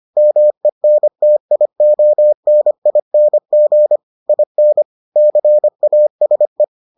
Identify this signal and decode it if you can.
Morse code, heard as a keyed tone, about 25 wpm, 600 Hz. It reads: MENTIONING IN CASE